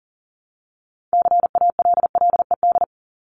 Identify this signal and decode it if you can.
Morse code, heard as a keyed tone, about 40 wpm, 700 Hz. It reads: CALLED
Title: ED Suffix - Top 100 Words with Base Words 40wpm